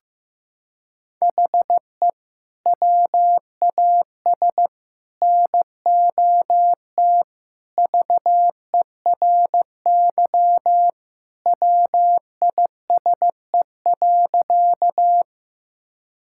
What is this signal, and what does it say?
Morse code, heard as a keyed tone, about 15 wpm, 700 Hz. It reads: HE WAS NOT VERY WISE.
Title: Sentences from Top 1600 Words 15wpm